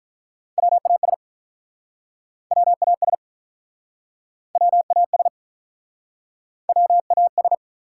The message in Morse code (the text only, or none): WAS WAS WAS WAS